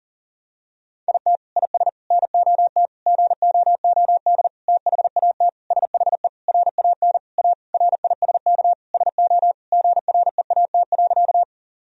Morse code, heard as a keyed tone, about 40 wpm, 700 Hz. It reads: IT IS NOT GOOD THAT SHE RAN A RISK SO GREAT.
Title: Sentences from Top 1000 Words - Review Part 4 - 40wpm